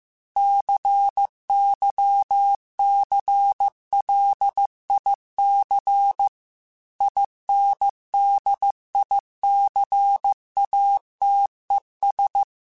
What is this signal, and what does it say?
Morse code, heard as a keyed tone, about 15 wpm, 800 Hz. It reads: CYCLIC INDICATES
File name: Technical Subjects / Technical Subjects-mp3s / XL @13